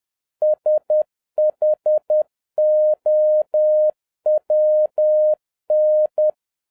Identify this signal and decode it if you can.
Morse code, heard as a keyed tone, about 10 wpm, 610 Hz. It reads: SHOWN
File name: 2011-07-mp3 / n41 EXCITING QRP CW OPERATION 10wpm